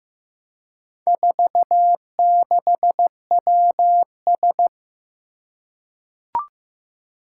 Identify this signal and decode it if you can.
Morse code, heard as a keyed tone, about 15 wpm, 700 Hz. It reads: 46WS
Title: Four Challenging Letter-Number Combinations 15wpm